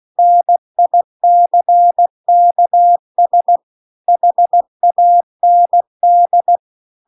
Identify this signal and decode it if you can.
Morse code, heard as a keyed tone, about 16 wpm, 700 Hz. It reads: NICKS HAND